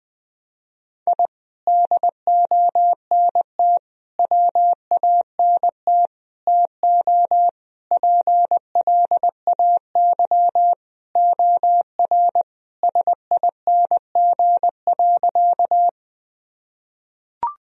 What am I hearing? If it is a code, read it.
Morse code, heard as a keyed tone, about 20 wpm, 700 Hz. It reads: I DONT WANT TO PLAY OR SING.
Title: Sentences from Top 1000 Words - Review Part 2 - 20wpm